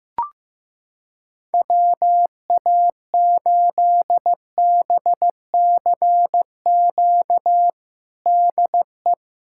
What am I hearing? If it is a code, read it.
Morse code, heard as a keyed tone, about 15 wpm, 700 Hz. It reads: WA8BCQ DE
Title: USA Callsign DE Callsign - 6 Characters - 15wpm